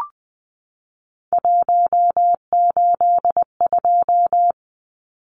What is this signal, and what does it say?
Morse code, heard as a keyed tone, about 19 wpm, 700 Hz. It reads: 182
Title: Numbers - 3-Digit 20wpm